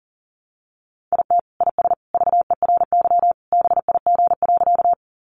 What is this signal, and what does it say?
Morse code, heard as a keyed tone, about 37 wpm, 700 Hz. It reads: IT IS VERY BIG.
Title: Easy Phrases 40wpm